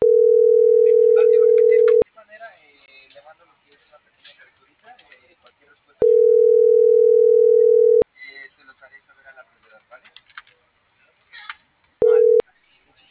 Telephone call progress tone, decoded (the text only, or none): ringback tone